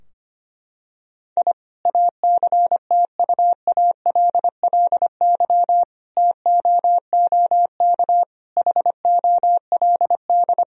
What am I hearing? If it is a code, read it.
Morse code, heard as a keyed tone, about 25 wpm, 700 Hz. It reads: I ACTUALLY TOOK HOLD
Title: Sentences from Top 1000 Words - Review Part 4 - 25wpm